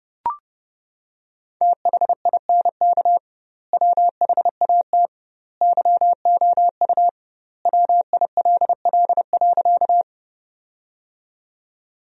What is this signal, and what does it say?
Morse code, heard as a keyed tone, about 32 wpm, 700 Hz. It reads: THINK WHAT YOU WILL.
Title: 2x Word-Spacing - Sentences from Top 200 Words - 30wpm